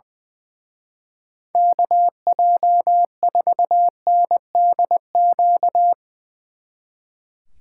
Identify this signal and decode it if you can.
Morse code, heard as a keyed tone, about 22 wpm, 700 Hz. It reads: KJ4NDQ
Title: Speed Racing - USA Call Signs 20wpm